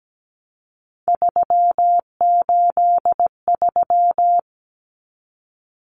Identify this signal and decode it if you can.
Morse code, heard as a keyed tone, about 17 wpm, 700 Hz. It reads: 383
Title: Numbers - 3-Digit 17wpm